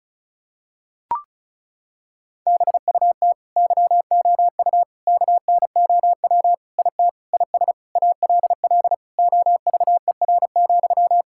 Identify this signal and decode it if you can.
Morse code, heard as a keyed tone, about 36 wpm, 700 Hz. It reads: BUT YOU KNOW IT IS ALL OVER,
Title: Sentences from Top 1600 Words 35wpm